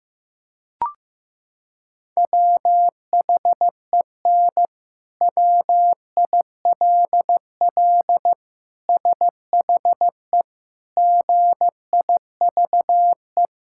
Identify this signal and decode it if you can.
Morse code, heard as a keyed tone, about 15 wpm, 700 Hz. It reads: WHEN WILL SHE GIVE
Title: Sentences from Top 1000 Words - Review Part 5 - 15wpm